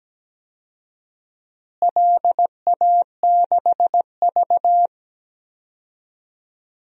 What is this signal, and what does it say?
Morse code, heard as a keyed tone, about 17 wpm, 700 Hz. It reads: LA6V
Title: DXCC-100 Call Signs - 4 Characters - Repeated 2x - 17wpm